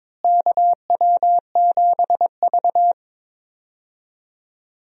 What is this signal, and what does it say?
Morse code, heard as a keyed tone, about 22 wpm, 700 Hz. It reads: KW7V
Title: USA Call Signs - 4 Characters - Repeated 2x - 22wpm